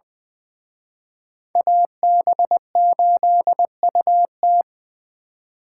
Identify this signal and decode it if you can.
Morse code, heard as a keyed tone, about 24 wpm, 700 Hz. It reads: AB8UT
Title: Speed Racing - USA Call Signs 20wpm